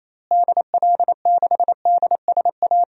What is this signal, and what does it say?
Morse code, heard as a keyed tone, about 28 wpm, 700 Hz. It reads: DL6DSA
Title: DXCC-100 Call Signs - 6 Characters - Repeated 2x - 28wpm